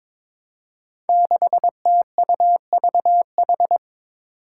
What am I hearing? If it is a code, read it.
Morse code, heard as a keyed tone, about 22 wpm, 700 Hz. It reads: 6TUVH
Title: Five Challenging Letter-Number Combinations 22wpm